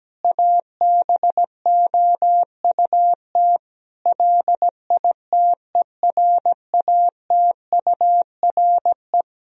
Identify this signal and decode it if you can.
Morse code, heard as a keyed tone, about 17 wpm, 700 Hz. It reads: ABOUT LITERATURE
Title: Sentences from Top 1000 Words - Review Part 3 - 17wpm